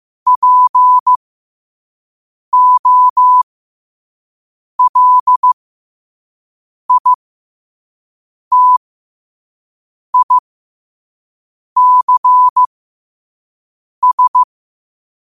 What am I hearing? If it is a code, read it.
Morse code, heard as a keyed tone, about 15 wpm, spaced as for 5 wpm, 1000 Hz. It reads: POLITICS
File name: qotdfiles / 5wpm